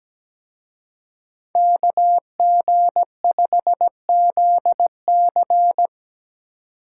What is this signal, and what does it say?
Morse code, heard as a keyed tone, about 20 wpm, 700 Hz. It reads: KG5ZC
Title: Speed Racing - USA 2x2 Call Signs 17wpm